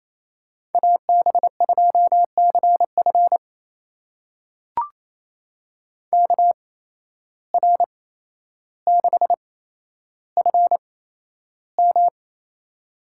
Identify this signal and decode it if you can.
Morse code, heard as a keyed tone, about 28 wpm, spaced as for 12 wpm, 700 Hz. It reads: AB2CF KR6FM
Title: USA Call Signs - 5 Characters - 8x Character Spacing - 28wpm